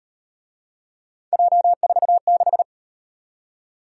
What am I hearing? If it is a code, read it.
Morse code, heard as a keyed tone, about 38 wpm, 700 Hz. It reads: J46